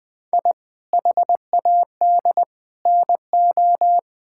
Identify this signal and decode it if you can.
Morse code, heard as a keyed tone, about 21 wpm, 700 Hz. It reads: I HAD NO
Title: Sentences from Top 1000 Words - Review Part 6 - 20wpm